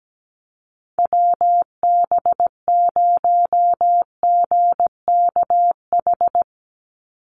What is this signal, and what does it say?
Morse code, heard as a keyed tone, about 17 wpm, 700 Hz. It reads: WB0GKH
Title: Call Signs Repeated 3x - 17wpm